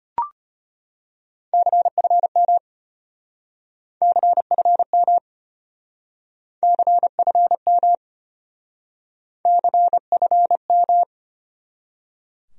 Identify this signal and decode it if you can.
Morse code, heard as a keyed tone, about 32 wpm, 700 Hz. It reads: CFM CFM CFM CFM